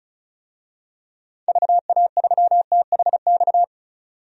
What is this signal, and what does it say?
Morse code, heard as a keyed tone, about 34 wpm, 700 Hz. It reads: VA3THX
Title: DXCC-100 Call Signs - 6 Characters - 35wpm